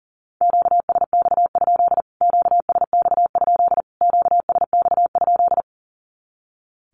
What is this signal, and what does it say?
Morse code, heard as a keyed tone, about 40 wpm, 700 Hz. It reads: QSX? QSX? QSX?
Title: Q-Codes Repeated 3x 40wpm